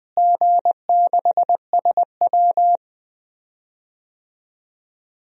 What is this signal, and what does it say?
Morse code, heard as a keyed tone, about 20 wpm, 700 Hz. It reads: G6SW
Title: DXCC-100 Call Signs - 4 Characters - Repeated 2x - 20wpm